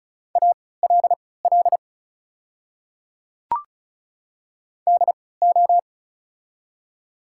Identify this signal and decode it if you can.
Morse code, heard as a keyed tone, about 33 wpm, spaced as for 19 wpm, 690 Hz. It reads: ALL DO
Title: 3x Character Spacing - Top 100 Words - 35wpm